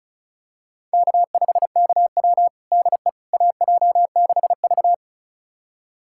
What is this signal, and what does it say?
Morse code, heard as a keyed tone, about 34 wpm, 690 Hz. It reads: K5KW DE AJ6V
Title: USA Callsign DE Callsign - 4 Characters - 35wpm